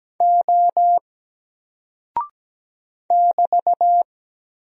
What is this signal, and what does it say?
Morse code, heard as a keyed tone, about 17 wpm, 700 Hz. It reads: O <BT>
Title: Single Character (Letter:Number:Punctuation:Prosign) Alt 17wpm